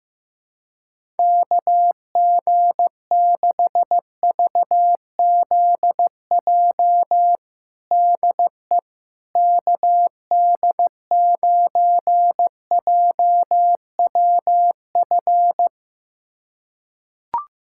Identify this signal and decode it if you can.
Morse code, heard as a keyed tone, about 15 wpm, 700 Hz. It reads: KG6VZJ DE KD9JWF
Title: USA Callsign DE Callsign - 6 Characters - 15wpm